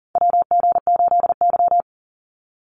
Morse code, heard as a keyed tone, about 39 wpm, 700 Hz. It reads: WG8Y
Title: NA SOTA Chaser Callsigns 40wpm